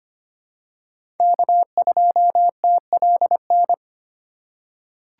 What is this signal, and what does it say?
Morse code, heard as a keyed tone, about 25 wpm, 700 Hz. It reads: K2TLN
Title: USA Call Signs - 5 Characters - 3x Character Spacing - 25wpm